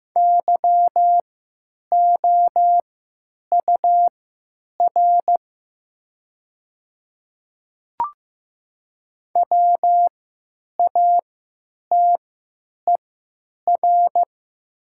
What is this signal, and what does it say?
Morse code, heard as a keyed tone, about 15 wpm, spaced as for 9 wpm, 700 Hz. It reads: YOUR WATER
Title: 3x Character Spacing - Top 200 Words - 15wpm